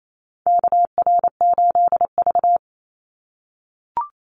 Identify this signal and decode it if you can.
Morse code, heard as a keyed tone, about 28 wpm, 700 Hz. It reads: KR8V
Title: Call Signs - 28wpm